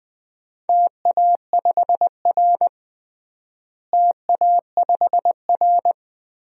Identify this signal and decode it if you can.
Morse code, heard as a keyed tone, about 20 wpm, 700 Hz. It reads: TA5R TA5R
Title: DXCC-100 Call Signs - 4 Characters - Repeated 2x - 20wpm